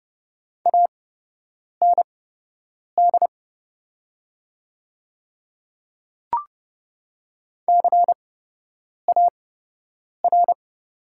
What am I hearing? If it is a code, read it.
Morse code, heard as a keyed tone, about 30 wpm, spaced as for 8 wpm, 700 Hz. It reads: AND CAR
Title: 8x Character Spacing - Top 500 Words - 30wpm